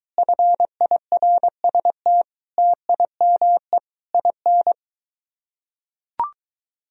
Morse code, heard as a keyed tone, about 23 wpm, 700 Hz. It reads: FIRST TIME IN